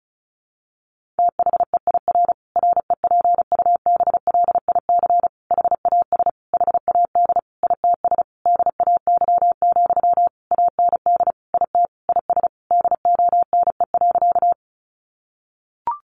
Morse code, heard as a keyed tone, about 34 wpm, 700 Hz. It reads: THEIR REPUBLIC HAS HAD ITS DAY, AND IT IS DONE.
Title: Sentences from Top 1500 Words 35wpm